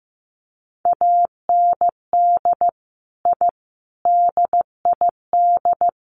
Morse code, heard as a keyed tone, about 15 wpm, 700 Hz. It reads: AND I DID
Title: Sets of 4 Words 15wpm